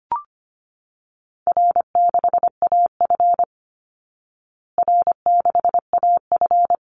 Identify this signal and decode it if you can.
Morse code, heard as a keyed tone, about 25 wpm, 700 Hz. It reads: R6AF R6AF
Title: Call Signs Repeated 2x - 25wpm